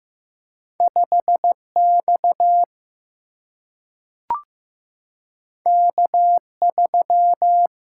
Morse code, heard as a keyed tone, about 15 wpm, 700 Hz. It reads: 5X K3
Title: Single-Letter Single-Number 15wpm